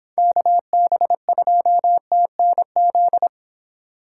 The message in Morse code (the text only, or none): KB2TNZ